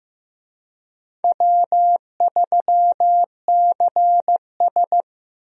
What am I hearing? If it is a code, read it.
Morse code, heard as a keyed tone, about 16 wpm, 700 Hz. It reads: W3CS
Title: USA Call Signs - 4 Characters - Repeated 2x - 15wpm